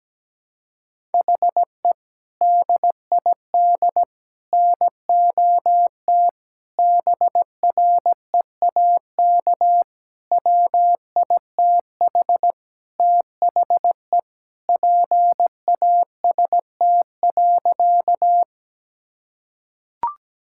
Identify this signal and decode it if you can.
Morse code, heard as a keyed tone, about 17 wpm, 700 Hz. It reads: HE DID NOT BREAK WITH THE PAST.
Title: Sentences from Top 1000 Words - Review Part 6 - 17wpm